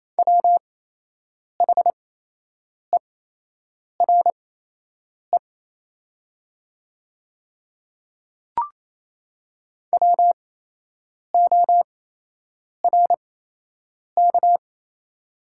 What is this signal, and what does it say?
Morse code, heard as a keyed tone, about 28 wpm, spaced as for 8 wpm, 700 Hz. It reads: WHERE WORK
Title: 8x Character Spacing - Top 200 Words - 28wpm